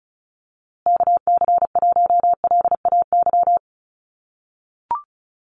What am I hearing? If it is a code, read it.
Morse code, heard as a keyed tone, about 34 wpm, 700 Hz. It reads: KC1LAY
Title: USA Call Signs Repeated 2x 35wpm